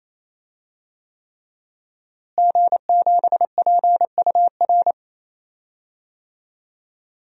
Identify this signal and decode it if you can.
Morse code, heard as a keyed tone, about 28 wpm, 700 Hz. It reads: G7PUR
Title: DXCC-100 Call Signs - 5 Characters - Repeated 3x - 28wpm